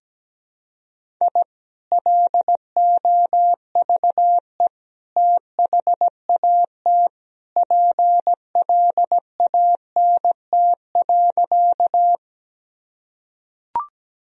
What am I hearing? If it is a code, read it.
Morse code, heard as a keyed tone, about 17 wpm, 700 Hz. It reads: I LOVE THAT PLANT.
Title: Sentences from Top 1000 Words - Review Part 1 - 17wpm